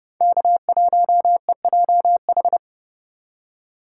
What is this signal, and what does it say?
Morse code, heard as a keyed tone, about 30 wpm, 700 Hz. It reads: K1EJH
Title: USA Call Signs - 5 Characters - 8x Character Spacing - 30wpm